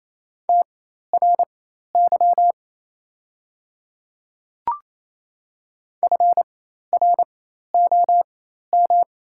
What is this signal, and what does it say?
Morse code, heard as a keyed tone, about 28 wpm, spaced as for 13 wpm, 690 Hz. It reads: TRY FROM
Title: 4x Character Spacing - Top 300 Words - 28wpm